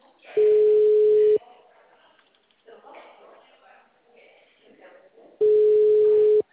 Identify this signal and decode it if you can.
Ringback tone (Europe, 425 Hz), with speech.
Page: Sorocaba - SP